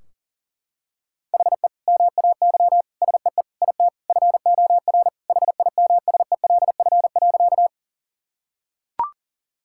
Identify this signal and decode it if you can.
Morse code, heard as a keyed tone, about 40 wpm, 700 Hz. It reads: HE MAY SEE IT FOR HIMSELF.
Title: Sentences from Top 1600 Words 40wpm